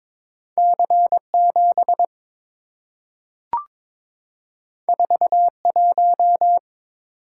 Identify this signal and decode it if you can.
Morse code, heard as a keyed tone, about 22 wpm, 700 Hz. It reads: C7 41
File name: practice / 22wpm / Two Letter-Number Combinations - 22wpm